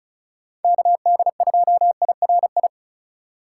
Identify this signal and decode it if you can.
Morse code, heard as a keyed tone, about 41 wpm, 700 Hz. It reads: KD2IRI